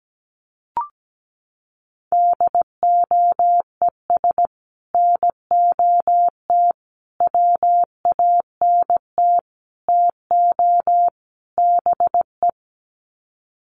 Morse code, heard as a keyed tone, about 17 wpm, 700 Hz. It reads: DOES NOT WANT TO BE
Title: Sets of 5 Words Encore 17wpm